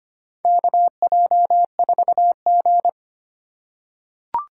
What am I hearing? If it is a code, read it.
Morse code, heard as a keyed tone, about 25 wpm, 700 Hz. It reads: KJ4G
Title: USA Call Signs - 4 Characters - 2x Character Spacing - 25wpm